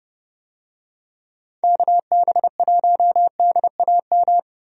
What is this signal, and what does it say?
Morse code, heard as a keyed tone, about 30 wpm, 700 Hz. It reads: KB1DAM
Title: USA Call Signs - 6 Characters - 30wpm